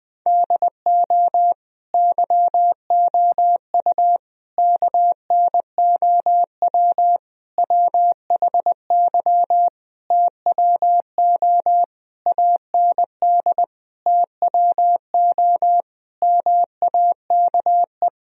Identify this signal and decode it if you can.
Morse code, heard as a keyed tone, about 21 wpm, 700 Hz. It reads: DO YOU KNOW WHY TWO AND TWO MAKE
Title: Sentences from Top 1000 Words - Review Part 1 - 20wpm